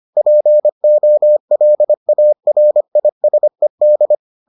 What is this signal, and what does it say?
Morse code, heard as a keyed tone, about 25 wpm, 600 Hz. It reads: POLARISED